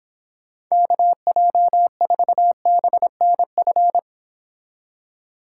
Morse code, heard as a keyed tone, about 26 wpm, 700 Hz. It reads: KJ4BNF